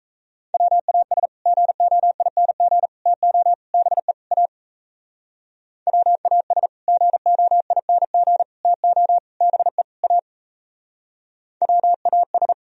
Morse code, heard as a keyed tone, about 36 wpm, 700 Hz. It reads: WAS GOING TO BE A WAS GOING TO BE A WAS